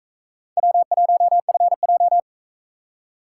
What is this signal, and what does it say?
Morse code, heard as a keyed tone, about 42 wpm, 700 Hz. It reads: W1FJ